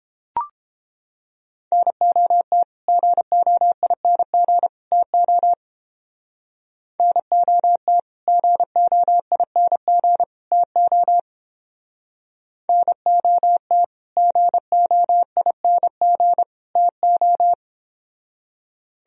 Morse code, heard as a keyed tone, about 29 wpm, 700 Hz. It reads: NOT GOING TO NOT GOING TO NOT GOING TO